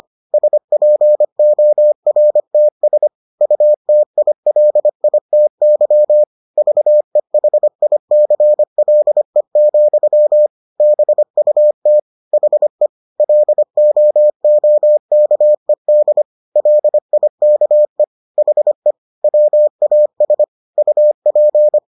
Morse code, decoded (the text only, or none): SPORTS UTILITY VEHICLE, BUT HE LOOKED LIKE HE WAS UP